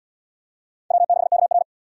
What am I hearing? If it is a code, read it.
Morse code, heard as a keyed tone, about 77 wpm, 700 Hz. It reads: W6KR